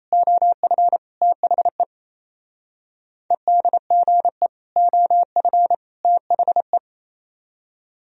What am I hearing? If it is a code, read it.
Morse code, heard as a keyed tone, about 34 wpm, 700 Hz. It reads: OF THE EDGE OF THE
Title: Speed-Racing - Sets of 3 Words 28wpm